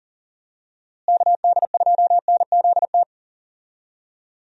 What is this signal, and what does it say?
Morse code, heard as a keyed tone, about 40 wpm, 700 Hz. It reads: KD2NZT
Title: USA Call Signs - 6 Characters - Repeated 2x - 40wpm